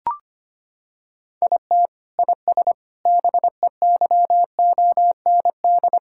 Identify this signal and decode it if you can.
Morse code, heard as a keyed tone, about 25 wpm, 700 Hz. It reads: IT IS BEYOND
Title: Sentences from Top 1000 Words - Review Part 6 - 25wpm